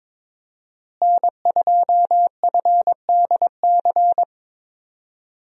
Morse code, heard as a keyed tone, about 22 wpm, 700 Hz. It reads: N2FDC
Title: USA Call Signs - 5 Characters - Repeated 2x - 22wpm